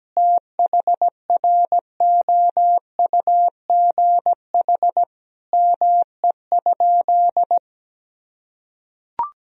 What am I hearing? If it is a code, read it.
Morse code, heard as a keyed tone, about 17 wpm, 700 Hz. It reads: THROUGH ME?
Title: Sentences from Top 1000 Words - Review Part 1 - 17wpm